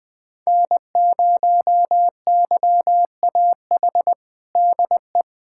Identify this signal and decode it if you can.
Morse code, heard as a keyed tone, about 20 wpm, 700 Hz. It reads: N0YAH DE
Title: USA Callsign DE Callsign - 5 Characters - 20wpm